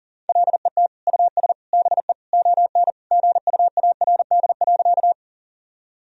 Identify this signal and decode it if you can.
Morse code, heard as a keyed tone, about 40 wpm, 700 Hz. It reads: LET US BE ON GUARD.
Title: Sentences from Top 1000 Words - Review Part 4 - 40wpm